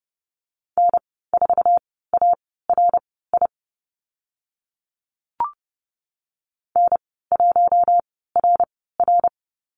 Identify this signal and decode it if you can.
Morse code, heard as a keyed tone, about 30 wpm, spaced as for 17 wpm, 700 Hz. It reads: N4ARI N1RR
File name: Lesson 14g - Number 2 - Call Signs 30wpm